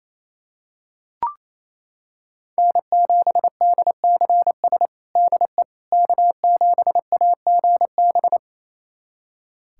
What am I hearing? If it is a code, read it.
Morse code, heard as a keyed tone, about 28 wpm, 700 Hz. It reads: N7DCS DE K7AGB
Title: USA Callsign DE Callsign 28wpm